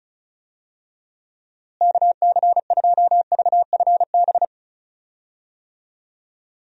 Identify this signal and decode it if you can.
Morse code, heard as a keyed tone, about 36 wpm, 700 Hz. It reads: KC2VFB
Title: USA Call Signs - 6 Characters - Repeated 2x - 35wpm